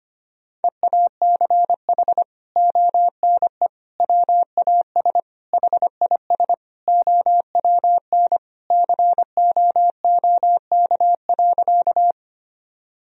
Morse code, heard as a keyed tone, about 25 wpm, 700 Hz. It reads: EACH ONE WAS HIS OWN COOK.
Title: Sentences from Top 1000 Words - Review Part 4 - 25wpm